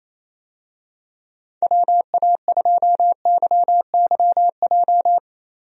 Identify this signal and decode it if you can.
Morse code, heard as a keyed tone, about 28 wpm, 700 Hz. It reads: WA2YYJ
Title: USA Call Signs - 6 Characters - 28wpm